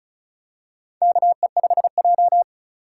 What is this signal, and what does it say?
Morse code, heard as a keyed tone, about 36 wpm, 700 Hz. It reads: KE5J
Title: USA Call Signs - 4 Characters - 3x Character Spacing - 35wpm